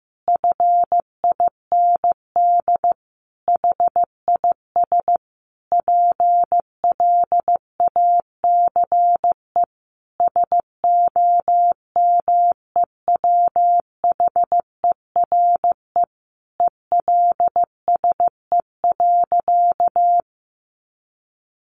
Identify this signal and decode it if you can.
Morse code, heard as a keyed tone, about 15 wpm, 700 Hz. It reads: FIND HIS PLACE SOMEWHERE ELSE.
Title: Sentences from Top 1400 Words 15wpm